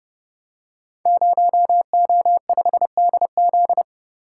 Morse code, heard as a keyed tone, about 30 wpm, 700 Hz. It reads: 0O5DZ
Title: Code Groups - 5 Letter:Numbers - 30wpm